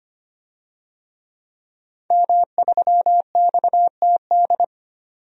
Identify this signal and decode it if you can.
Morse code, heard as a keyed tone, about 25 wpm, 700 Hz. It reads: M3XTD
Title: DXCC-100 Call Signs - 5 Characters - 25wpm